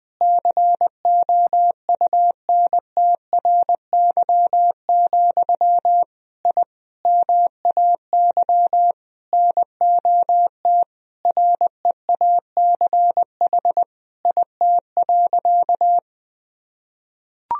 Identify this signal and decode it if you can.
Morse code, heard as a keyed tone, about 20 wpm, 700 Hz. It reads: COUNTRY, I MAY NOT REACH IT.
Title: Sentences from Top 1000 Words - Review Part 2 - 20wpm